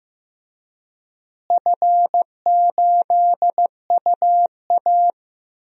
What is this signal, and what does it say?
Morse code, heard as a keyed tone, about 15 wpm, 700 Hz. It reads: F8UA
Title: DXCC-100 Call Signs - 4 Characters - Repeated 3x - 15wpm